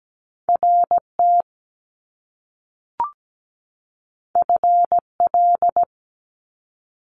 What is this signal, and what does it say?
Morse code, heard as a keyed tone, about 17 wpm, 700 Hz. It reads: RT FL
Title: Top Two Letter Combinations in Words 17wpm